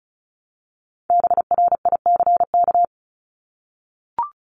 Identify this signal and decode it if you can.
Morse code, heard as a keyed tone, about 34 wpm, 700 Hz. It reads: BRICK
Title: Top 1751-2000 Words 35wpm